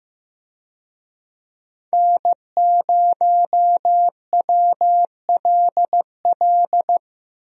Morse code, heard as a keyed tone, about 15 wpm, 700 Hz. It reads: N0WLL
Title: USA Call Signs - 5 Characters - 15wpm